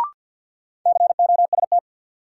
Morse code, heard as a keyed tone, about 53 wpm, 700 Hz. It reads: COST
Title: Sentences from Top 800 Words - New Word Drill - 50wpm